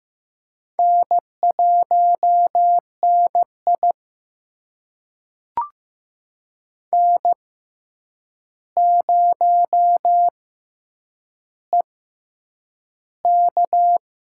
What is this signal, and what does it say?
Morse code, heard as a keyed tone, about 15 wpm, spaced as for 8 wpm, 700 Hz. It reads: N1NI N0EK
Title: USA Call Signs - 4 Characters - 6x Character Spacing - 15wpm